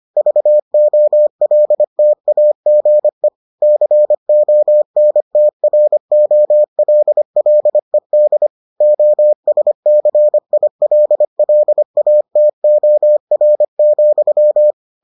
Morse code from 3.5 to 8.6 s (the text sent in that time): CONTROLLED